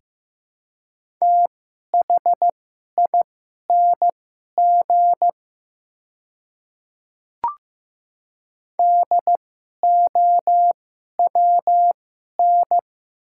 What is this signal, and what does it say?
Morse code, heard as a keyed tone, about 15 wpm, spaced as for 11 wpm, 700 Hz. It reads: THING DOWN
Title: 2x Character Spacing - Top 100 Words - 15wpm